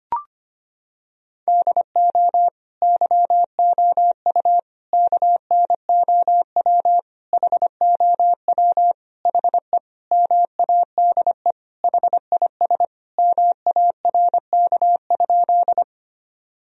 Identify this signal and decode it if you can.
Morse code, heard as a keyed tone, about 25 wpm, 700 Hz. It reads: DO YOU KNOW HOW HE MADE HIS MARK?
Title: Sentences from Top 1000 Words - Review Part 1 - 25wpm